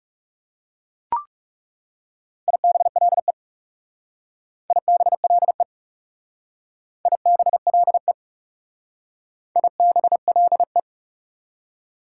Morse code, heard as a keyed tone, about 36 wpm, 700 Hz. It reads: IBLE IBLE IBLE IBLE
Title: Speed-Racing Prefixes and Suffixes 30wpm